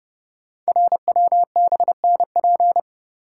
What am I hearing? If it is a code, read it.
Morse code, heard as a keyed tone, about 30 wpm, 700 Hz. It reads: RWBNP
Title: Code Groups - 5 Letters - 30wpm